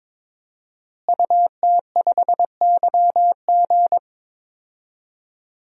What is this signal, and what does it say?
Morse code, heard as a keyed tone, about 21 wpm, 700 Hz. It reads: UT5YG
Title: DXCC-100 Call Signs - 5 Characters - Repeated 3x - 22wpm